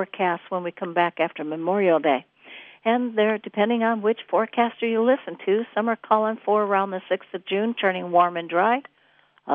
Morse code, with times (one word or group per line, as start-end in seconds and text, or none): none